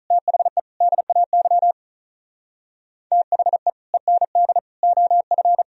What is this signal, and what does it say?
Morse code, heard as a keyed tone, about 46 wpm, 700 Hz. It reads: THE DAY THE END OF